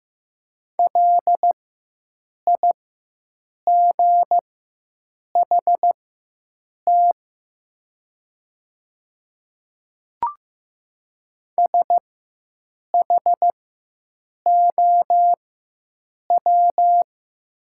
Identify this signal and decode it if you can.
Morse code, heard as a keyed tone, about 15 wpm, spaced as for 7 wpm, 700 Hz. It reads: LIGHT SHOW
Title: 4x Character Spacing - Top 200 Words - 15wpm